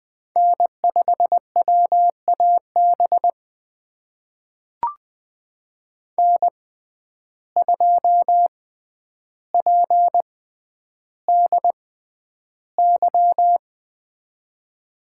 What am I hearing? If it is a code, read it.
Morse code, heard as a keyed tone, about 20 wpm, spaced as for 10 wpm, 700 Hz. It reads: N5WAB N2PDY
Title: USA Call Signs - 5 Characters - 6x Character Spacing - 20wpm